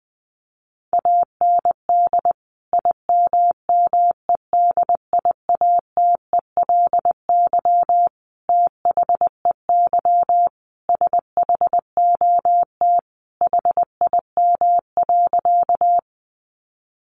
Morse code, heard as a keyed tone, about 19 wpm, 700 Hz. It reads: AND IMMEDIATELY THEY SHOT HIM.
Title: Sentences from Top 1100 Words 20wpm